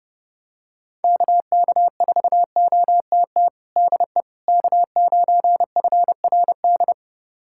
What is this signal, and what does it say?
Morse code, heard as a keyed tone, about 30 wpm, 700 Hz. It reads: KK4OTT DE K9FRD
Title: USA Callsign DE Callsign 30wpm